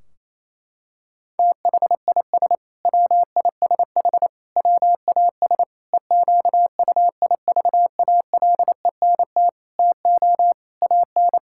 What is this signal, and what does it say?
Morse code, heard as a keyed tone, about 28 wpm, 700 Hz. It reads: THIS WISH WAS EQUIVALENT TO AN
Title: Sentences from Top 1600 Words 28wpm